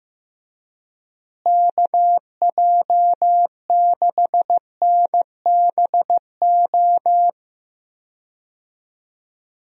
Morse code, heard as a keyed tone, about 15 wpm, 700 Hz. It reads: KJ6NBO
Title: USA Call Signs - 6 Characters - Repeated 2x - 15wpm